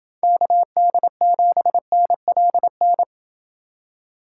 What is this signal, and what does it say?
Morse code, heard as a keyed tone, about 28 wpm, 700 Hz. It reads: KD7NLN